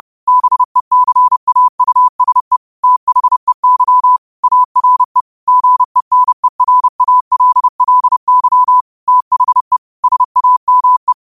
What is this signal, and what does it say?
Morse code, heard as a keyed tone, about 30 wpm, 1000 Hz. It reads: BECAUSE THEY ARE GENERALLY THE SAME